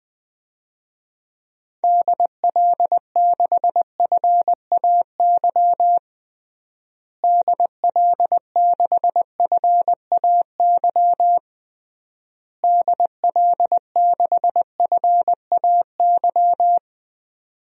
Morse code, heard as a keyed tone, about 20 wpm, 700 Hz. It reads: DL6FAY DL6FAY DL6FAY
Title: DXCC-100 Call Signs - 6 Characters - Repeated 3x - 20wpm